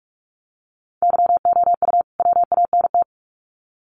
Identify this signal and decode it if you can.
Morse code, heard as a keyed tone, about 46 wpm, 700 Hz. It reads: YOU WANT
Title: Sets of 2 Words 45wpm